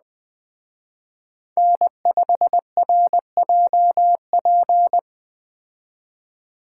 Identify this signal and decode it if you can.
Morse code, heard as a keyed tone, about 20 wpm, 700 Hz. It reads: N5RJP
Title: Speed Racing - USA 1x3 Call Signs 17wpm